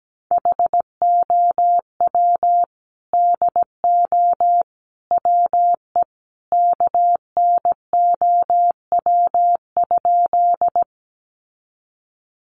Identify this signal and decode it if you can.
Morse code, heard as a keyed tone, about 17 wpm, 700 Hz. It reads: HOW DO WE KNOW?
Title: A Sound Of Thunder 17wpm